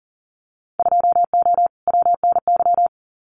Morse code, heard as a keyed tone, about 40 wpm, 700 Hz. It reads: 2O WNY
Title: Winter Field Day Exchanges Encore 40wpm